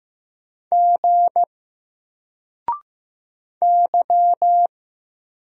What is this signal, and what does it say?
Morse code, heard as a keyed tone, about 15 wpm, 700 Hz. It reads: G Y
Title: Single Letter-Number 15wpm